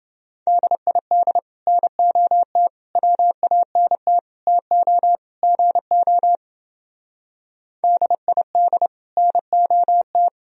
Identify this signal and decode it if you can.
Morse code, heard as a keyed tone, about 29 wpm, 700 Hz. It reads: DID NOT WANT TO GO DID NOT